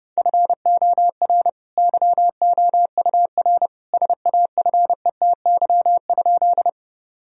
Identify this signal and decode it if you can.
Morse code, heard as a keyed tone, about 30 wpm, 700 Hz. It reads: FOR YOUR SAFETY?
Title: Sentences from Top 1000 Words - Review Part 4 - 30wpm